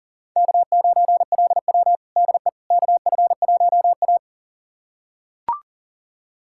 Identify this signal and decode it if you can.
Morse code, heard as a keyed tone, about 40 wpm, 700 Hz. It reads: K9LW DE KF1A
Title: USA Callsign DE Callsign - 4 Characters - 40wpm